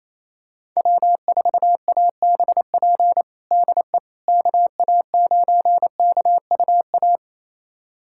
Morse code, heard as a keyed tone, about 28 wpm, 700 Hz. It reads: W4ABP DE KA9KUA
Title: USA Callsign DE Callsign 28wpm